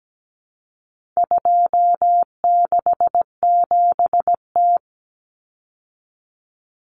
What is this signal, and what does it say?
Morse code, heard as a keyed tone, about 17 wpm, 700 Hz. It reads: 267T
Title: Common Cut Numbers - 4-Digit 17wpm